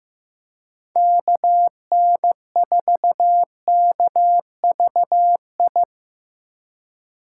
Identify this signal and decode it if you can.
Morse code, heard as a keyed tone, about 18 wpm, 700 Hz. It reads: KN4KVI
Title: Speed Racing - USA 2x3 Call Signs 15wpm